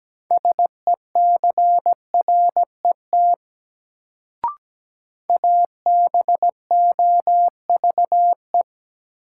Morse code, heard as a keyed tone, about 17 wpm, 700 Hz. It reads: SECRET ABOVE
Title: Sentences from Top 1200 Words - New Word Drill Alt - 17wpm